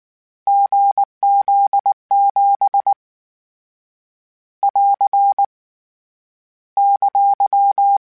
Morse code, heard as a keyed tone, about 19 wpm, 800 Hz. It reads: GZ7 <AR> !